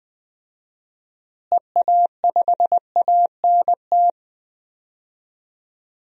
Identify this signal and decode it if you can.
Morse code, heard as a keyed tone, about 21 wpm, 700 Hz. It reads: EA5ANT
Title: DXCC-100 Call Signs - 6 Characters - Repeated 2x - 20wpm